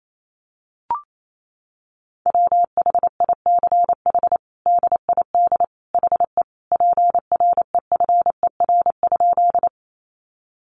Morse code, heard as a keyed tone, about 28 wpm, 700 Hz. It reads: WHICH DID HE PREFER?
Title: Sentences from Top 1100 Words 28wpm